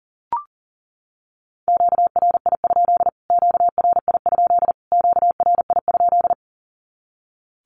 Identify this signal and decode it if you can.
Morse code, heard as a keyed tone, about 40 wpm, 700 Hz. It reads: QRI? QRI? QRI?
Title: Q-Codes Repeated 3x 40wpm